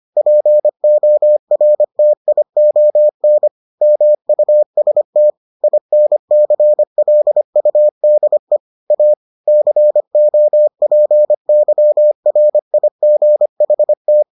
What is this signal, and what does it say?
Morse code, heard as a keyed tone, about 25 wpm, 600 Hz. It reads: PORTION MUST INCLUDE A COPYRIGHT